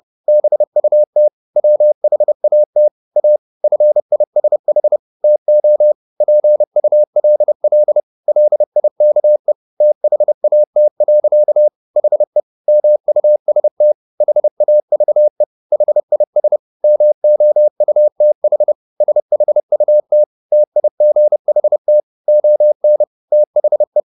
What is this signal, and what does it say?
Morse code, heard as a keyed tone, about 30 wpm, 600 Hz. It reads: BUT WHAT A FISH TO PULL LIKE THAT. HE MUST HAVE HIS MOUTH SHUT TIGHT ON THE
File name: the old man and the sea / the old man and the sea 1 30wpm